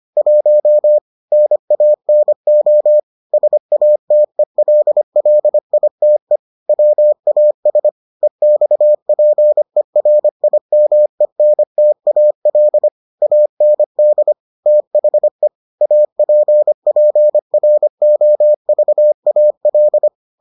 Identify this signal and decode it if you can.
Morse code, heard as a keyed tone, about 25 wpm, 600 Hz. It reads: 1 NANO SATELLITE WAS EXPERIMENTAL AND THE APPROVAL